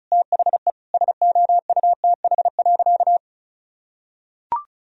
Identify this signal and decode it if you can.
Morse code, heard as a keyed tone, about 36 wpm, 700 Hz. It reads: THE SOUTH.
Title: Sentences from Top 400 Words and Abbreviations 35wpm